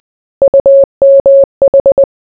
Morse code, heard as a keyed tone, about 20 wpm, 550 Hz. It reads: UMH